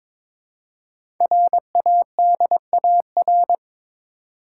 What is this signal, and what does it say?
Morse code, heard as a keyed tone, about 22 wpm, 700 Hz. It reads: RADAR
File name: 5 Letter Words - First 3 Letters Given - 22wpm